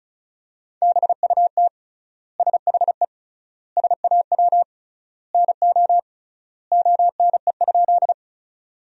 Morse code, heard as a keyed tone, about 34 wpm, 700 Hz. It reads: BUT SHE SAW NO ONE?
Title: 3x Word-Spacing - Sentences from Top 300 Words - 35wpm